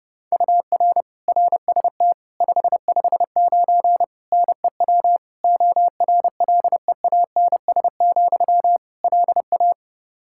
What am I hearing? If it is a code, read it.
Morse code, heard as a keyed tone, about 31 wpm, 700 Hz. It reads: UR RST 559 NEW ORLEANS, LA
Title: Practice QSO - Exchange 1 - Beginning - 30wpm